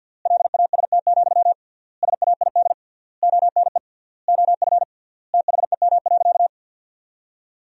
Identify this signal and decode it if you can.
Morse code, heard as a keyed tone, about 50 wpm, 700 Hz. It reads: LAST, SAID ONE OF THEM.
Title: 3x Word-Spacing - Sentences from Top 300 Words - 50wpm